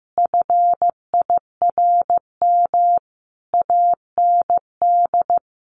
Morse code, heard as a keyed tone, about 15 wpm, 700 Hz. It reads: FIRM AND
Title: Sentences from Top 1400 Words 15wpm